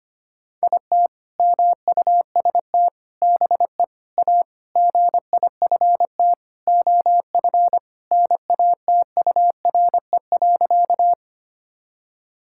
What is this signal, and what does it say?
Morse code, heard as a keyed tone, about 25 wpm, 700 Hz. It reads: IT MUST BE A GIFT OF NATURE.
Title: Sentences from Top 1000 Words - Review Part 6 - 25wpm